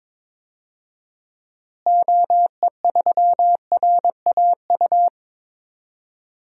Morse code, heard as a keyed tone, about 22 wpm, 700 Hz. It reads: OE3RAU
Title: DXCC-100 Call Signs - 6 Characters - Repeated 3x - 22wpm